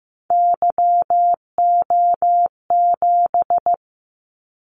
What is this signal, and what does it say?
Morse code, heard as a keyed tone, about 15 wpm, 700 Hz. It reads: YO7
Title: Callsign Prefixes for Top 25 DXCC - 15wpm